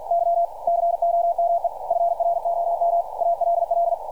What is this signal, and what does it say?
Morse code, heard as a keyed tone, about 52 wpm, 690 Hz. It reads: Y COME FROM ALL
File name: archived / CRSnet